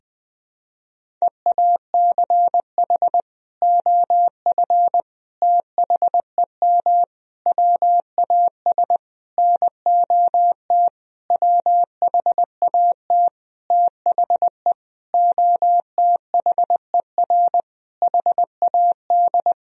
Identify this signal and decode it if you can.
Morse code, heard as a keyed tone, about 21 wpm, 700 Hz. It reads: EACH OF THEM WAS NOT WHAT THE OTHER HAD
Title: Sentences from Top 1000 Words - Review Part 1 - 20wpm